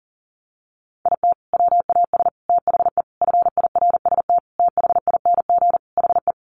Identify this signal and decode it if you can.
Morse code, heard as a keyed tone, about 40 wpm, 700 Hz. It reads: IT WAS THE FIRST THING HE
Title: Sentences from Top 300 Words 40wpm